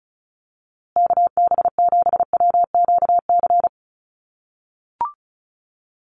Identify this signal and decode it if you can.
Morse code, heard as a keyed tone, about 34 wpm, 700 Hz. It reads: KB7WQC
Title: USA Call Signs Repeated 2x 35wpm